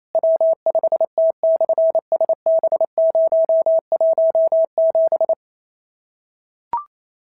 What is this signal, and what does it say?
Morse code, heard as a keyed tone, about 28 wpm, 690 Hz. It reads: W5T/SB017
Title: SOTA QSO Practice 28wpm